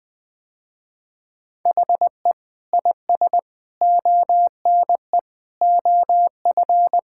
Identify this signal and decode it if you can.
Morse code, heard as a keyed tone, about 20 wpm, 700 Hz. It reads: HE IS ONE OF
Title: Sentences with 3-Word Phrases - Spoken Beginning - 20wpm